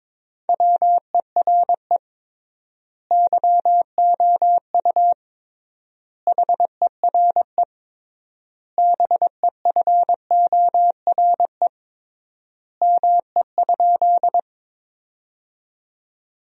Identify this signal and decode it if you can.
Morse code, heard as a keyed tone, about 22 wpm, 700 Hz. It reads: WERE YOU HERE BEFORE ME?
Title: 3x Word-Spacing - Sentences from Top 200 Words - 22wpm